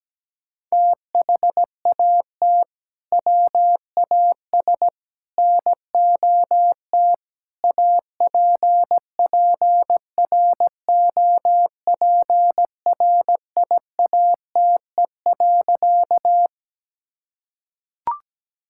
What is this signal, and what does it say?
Morse code, heard as a keyed tone, about 17 wpm, 700 Hz. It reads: THAT WAS NOT APPROPRIATE.
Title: Sentences from Top 1000 Words - Review Part 5 - 17wpm